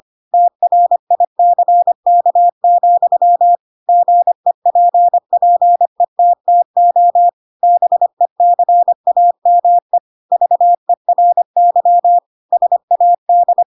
Morse code, heard as a keyed tone, about 25 wpm, 700 Hz. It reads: TRICK, GEPPETTO BECAME VERY SAD
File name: Pinocchio / Pinocchio ch3 25 25 ik1hge